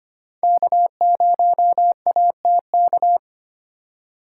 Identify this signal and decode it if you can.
Morse code, heard as a keyed tone, about 32 wpm, 700 Hz. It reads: K0ATK